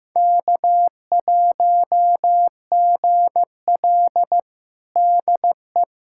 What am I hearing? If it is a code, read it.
Morse code, heard as a keyed tone, about 15 wpm, 700 Hz. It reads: K1GL DE
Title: USA Callsign DE Callsign - 4 Characters - 15wpm